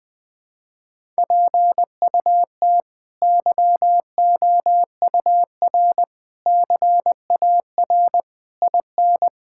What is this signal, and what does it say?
Morse code, heard as a keyed tone, about 21 wpm, 700 Hz. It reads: PUT YOUR CAR IN
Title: Sentences from Top 1600 Words 20wpm